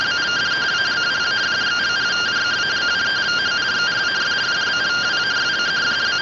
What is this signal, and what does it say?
Two-tone teleprinter signal, about 170 Hz shift.